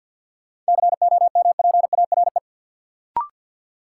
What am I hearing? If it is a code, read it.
Morse code, heard as a keyed tone, about 51 wpm, 700 Hz. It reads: COMPARE
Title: Sentences from Top 1300 Words - New Word Drill - 50wpm